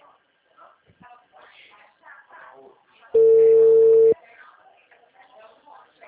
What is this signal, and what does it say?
Ringback tone (China, 450 Hz), with speech.
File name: out- -2002- - - .98382